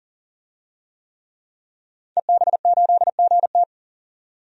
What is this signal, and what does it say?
Morse code, heard as a keyed tone, about 40 wpm, 700 Hz. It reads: EB8GT
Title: DXCC-100 Call Signs - 5 Characters - 40wpm